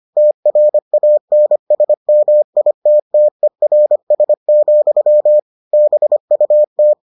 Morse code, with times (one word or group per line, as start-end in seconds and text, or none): 0.2-5.4 TRANSMITTERS,
5.7-6.9 BUT